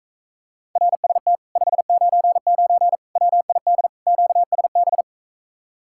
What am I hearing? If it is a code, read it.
Morse code, heard as a keyed tone, about 45 wpm, 700 Hz. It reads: RST 599 WID QSB